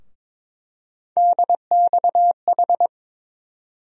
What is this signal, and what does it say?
Morse code, heard as a keyed tone, about 22 wpm, 700 Hz. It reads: DXH
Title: USA Call Sign Postfixes - 3 Letters 22wpm